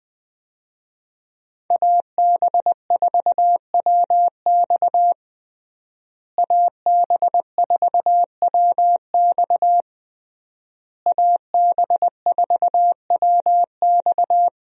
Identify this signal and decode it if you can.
Morse code, heard as a keyed tone, about 20 wpm, 700 Hz. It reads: AB4WX AB4WX AB4WX
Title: USA Call Signs - 5 Characters - Repeated 3x - 20wpm